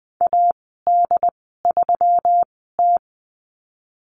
Morse code, heard as a keyed tone, about 20 wpm, spaced as for 15 wpm, 700 Hz. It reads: AD3T
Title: 122 Lesson 20g - Number 3 - Call Signs 20wpm